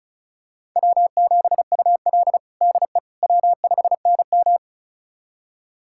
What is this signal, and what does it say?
Morse code, heard as a keyed tone, about 36 wpm, 700 Hz. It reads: W7UL DE W5NM
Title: USA Callsign DE Callsign - 4 Characters - 35wpm